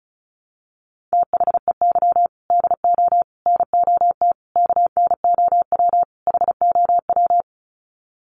Morse code, heard as a keyed tone, about 34 wpm, 700 Hz. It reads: THEY DO NOT KNOW HOW
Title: Sets of 5 Words Encore2 35wpm